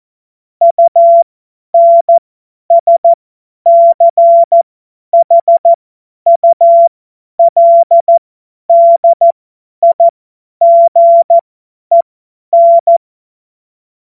Morse code, heard as a keyed tone, about 14 wpm, spaced as for 10 wpm, 680 Hz. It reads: UNSCHULDIGEN
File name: Morsekurs / Witze / Witz 09. Dezember 2025 50 F70